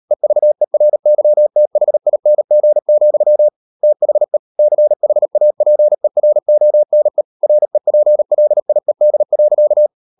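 Morse code, heard as a keyed tone, about 38 wpm, 600 Hz. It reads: EVERYTHING, THE CHAPERONE REPLIED.
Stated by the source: Morse code practice at 38 WPM, sending Ken Follett's novel Never in English, from CW med Gnister.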